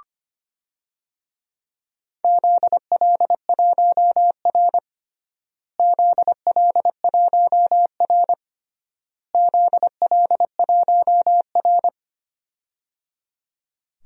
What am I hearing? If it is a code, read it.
Morse code, heard as a keyed tone, about 25 wpm, 700 Hz. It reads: ZL1R ZL1R ZL1R
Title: DXCC-100 Call Signs - 4 Characters - Repeated 3x - 25wpm